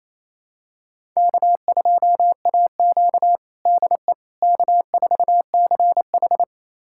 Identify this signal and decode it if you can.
Morse code, heard as a keyed tone, about 28 wpm, 700 Hz. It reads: K2AQ DE K4CH
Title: USA Callsign DE Callsign - 4 Characters - 28wpm